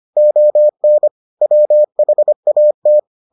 Morse code, heard as a keyed tone, about 25 wpm, 600 Hz. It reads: ON WHAT